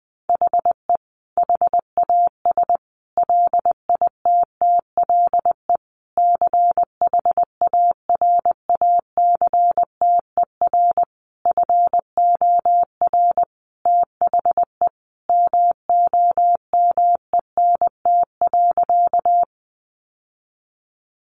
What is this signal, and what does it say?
Morse code, heard as a keyed tone, about 20 wpm, 700 Hz. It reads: HE HAS LITTLE CHARACTER FOR THE MOMENT.
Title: Sentences from Top 1200 Words 20wpm